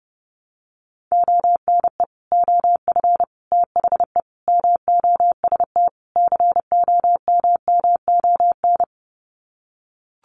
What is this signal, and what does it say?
Morse code, heard as a keyed tone, about 30 wpm, 700 Hz. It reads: ONE OF THE MOST COMMON
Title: Sets of 5 Words Encore2 30wpm